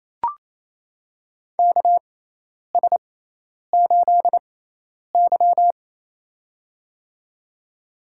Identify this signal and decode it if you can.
Morse code, heard as a keyed tone, about 28 wpm, spaced as for 10 wpm, 700 Hz. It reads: KS8Y
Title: USA Call Signs - 4 Characters - 6x Character Spacing - 28wpm